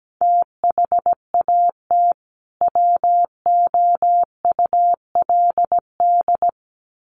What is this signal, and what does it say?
Morse code, heard as a keyed tone, about 17 wpm, 700 Hz. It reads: THAT WOULD
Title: Sets of 2 Words 17wpm